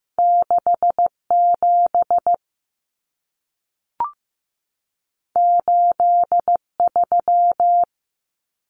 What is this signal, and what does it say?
Morse code, heard as a keyed tone, about 15 wpm, 700 Hz. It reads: 67 83
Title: Numbers - 2-Digit 15wpm